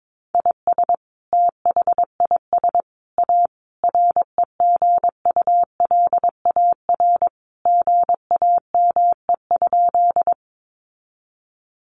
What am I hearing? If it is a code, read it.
Morse code, heard as a keyed tone, about 22 wpm, 700 Hz. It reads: IS THIS A REGULAR GAME?
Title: Sentences from Top 900 Words 22wpm